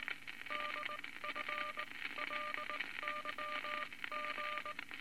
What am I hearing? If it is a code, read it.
Morse code, heard as a keyed tone, about 19 wpm, spaced as for 16 wpm, 1200 Hz, with noise. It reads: DFLYG
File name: Morse Code Records / 19 B LCM 167 Twelve Words Per Minute